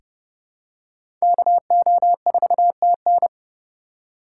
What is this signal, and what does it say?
Morse code, heard as a keyed tone, about 30 wpm, 700 Hz. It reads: KO4TN
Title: USA Call Signs - 5 Characters - 2x Character Spacing - 30wpm